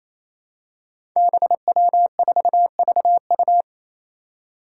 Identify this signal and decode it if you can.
Morse code, heard as a keyed tone, about 28 wpm, 700 Hz. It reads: BW4VU
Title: Five Challenging Letter-Number Combinations 28wpm